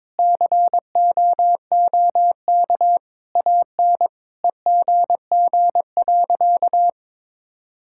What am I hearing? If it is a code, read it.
Morse code, heard as a keyed tone, about 22 wpm, 700 Hz. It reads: COOK AN EGG.
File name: Sentences with 7-Word Phrases - 22wpm